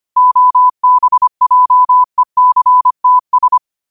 Morse code, heard as a keyed tone, about 25 wpm, 1000 Hz. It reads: OBJECTS
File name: 25wpm